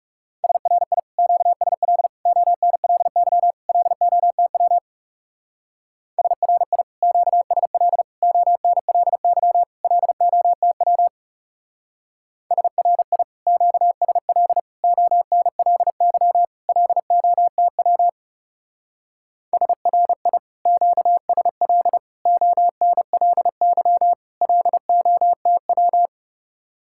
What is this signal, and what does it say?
Morse code, heard as a keyed tone, about 36 wpm, 700 Hz. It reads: SRI QSL ONLY LOTW SRI QSL ONLY LOTW SRI QSL ONLY LOTW SRI QSL ONLY LOTW
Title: Speed-Racing Common QSO Phrases - 30wpm